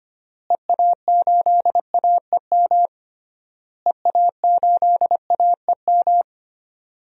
Morse code, heard as a keyed tone, about 25 wpm, 700 Hz. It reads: EA8AEM EA8AEM
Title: DXCC-100 Call Signs - 6 Characters - Repeated 3x - 25wpm